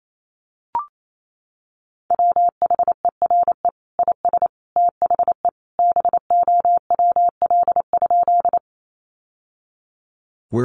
Morse code, heard as a keyed tone, about 28 wpm, 700 Hz. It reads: WHERE IS THE BOWL?
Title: Sentences from Top 1200 Words 28wpm